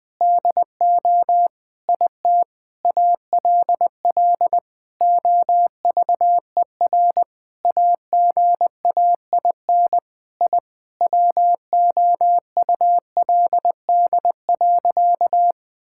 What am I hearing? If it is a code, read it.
Morse code, heard as a keyed tone, about 19 wpm, 700 Hz. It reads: DO IT ALL OVER AGAIN I WOULD.
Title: Sentences with 9-Word Phrases - Spoken Beginning - 20wpm